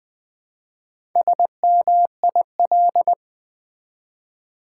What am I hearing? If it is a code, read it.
Morse code, heard as a keyed tone, about 20 wpm, 700 Hz. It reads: SMIL
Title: Build A Word - 1 Letter At A Time - 5-Letter Words - 20wpm